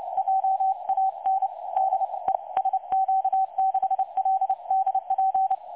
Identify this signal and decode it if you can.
Morse code, heard as a keyed tone, about 29 wpm, 770 Hz. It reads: JAN DE SQ6LDP
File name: dlazaawansowanych / nagraniazpasma / websdr recording 3565.0kHz